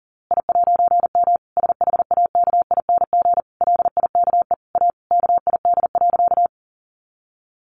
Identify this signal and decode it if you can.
Morse code, heard as a keyed tone, about 40 wpm, 700 Hz. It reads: I'M SHAKING LIKE A KID.
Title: A Sound Of Thunder 40wpm